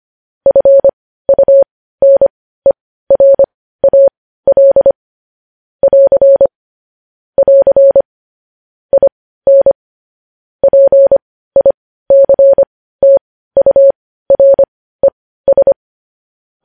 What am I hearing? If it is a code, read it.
Morse code, heard as a keyed tone, about 25 wpm, spaced as for 15 wpm, 550 Hz. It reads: FUNERAL <AR> <AR> IN PICTURES
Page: News headlines in morse code at 15 wpm